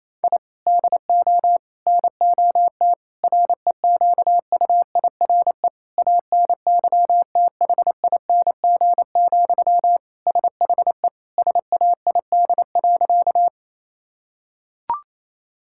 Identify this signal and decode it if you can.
Morse code, heard as a keyed tone, about 28 wpm, 700 Hz. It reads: I DO NOT REQUIRE ANYTHING, SHE SAID.
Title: Sentences from Top 1000 Words - Review Part 6 - 28wpm